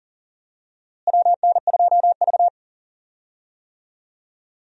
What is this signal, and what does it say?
Morse code, heard as a keyed tone, about 40 wpm, 700 Hz. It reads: WN2V
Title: USA Call Signs - 4 Characters - Repeated 3x - 40wpm